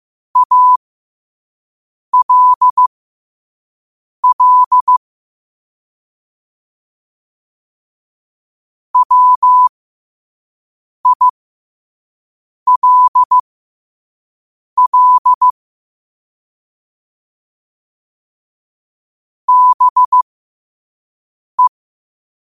Morse code, heard as a keyed tone, about 15 wpm, spaced as for 5 wpm, 1000 Hz. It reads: ALL WILL BE
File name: qotdfiles / 5wpm